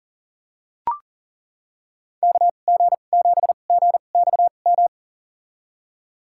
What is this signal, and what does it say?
Morse code, heard as a keyed tone, about 40 wpm, spaced as for 29 wpm, 690 Hz. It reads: KG7GXM
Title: USA Call Signs - 6 Characters - 2x Character Spacing - 40wpm